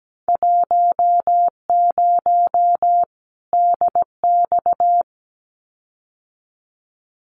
Morse code, heard as a keyed tone, about 17 wpm, 700 Hz. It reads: 10 DX
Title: Winter Field Day Exchanges 17wpm